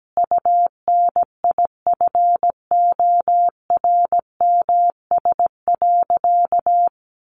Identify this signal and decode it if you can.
Morse code, heard as a keyed tone, about 17 wpm, 700 Hz. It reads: UNIFORMS.
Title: Easy Phrases 17wpm